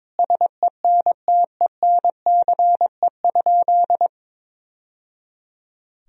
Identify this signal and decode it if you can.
Morse code, heard as a keyed tone, about 21 wpm, 700 Hz. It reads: SENTENCE?
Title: Sentences from Top 1000 Words - Review Part 1 - 22wpm